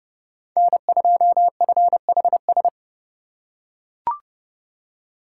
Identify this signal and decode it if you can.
Morse code, heard as a keyed tone, about 30 wpm, 700 Hz. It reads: N2FHS
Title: USA Call Signs - 5 Characters - 6x Character Spacing - 30wpm